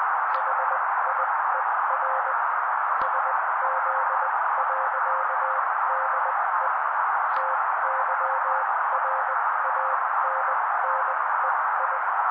Morse code, heard as a keyed tone, about 20 wpm, spaced as for 16 wpm, 530 Hz. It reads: HIER SZ.DE TYRANNEI
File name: daily / 16wpm noise